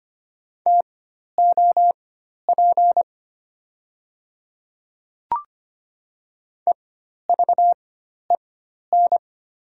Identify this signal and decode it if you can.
Morse code, heard as a keyed tone, about 25 wpm, spaced as for 12 wpm, 700 Hz. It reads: TOP EVEN
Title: 4x Character Spacing - Top 500 Words - 25wpm